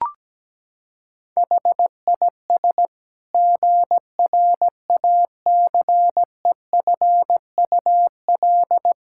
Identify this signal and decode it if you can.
Morse code, heard as a keyed tone, about 17 wpm, 700 Hz. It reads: HIS GRACEFUL
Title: Pangram Challenge v2 - 17wpm